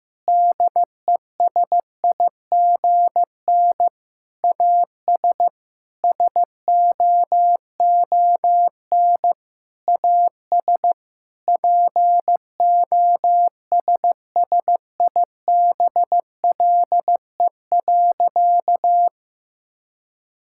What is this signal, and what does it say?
Morse code, heard as a keyed tone, about 15 wpm, 700 Hz. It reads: DESIGN AS SOON AS POSSIBLE.
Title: Sentences from Top 1000 Words - Review Part 5 - 15wpm